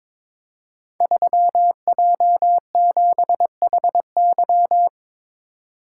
Morse code, heard as a keyed tone, about 22 wpm, 700 Hz. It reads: 3J7HY
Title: Code Groups - 5 Letter:Numbers - 22wpm